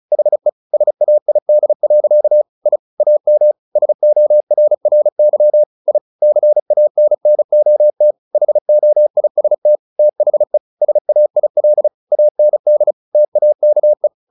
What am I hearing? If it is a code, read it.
Morse code, heard as a keyed tone, about 34 wpm, 600 Hz. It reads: HE SAID. I AM SORRY I CANNOT HOIST THE SAIL AND TAKE
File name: the old man and the sea / the old man and the sea 1 35wpm